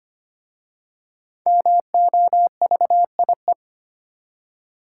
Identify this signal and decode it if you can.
Morse code, heard as a keyed tone, about 25 wpm, 700 Hz. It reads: MOVIE
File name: practice / 25wpm / Easy Words with Hints - 25wpm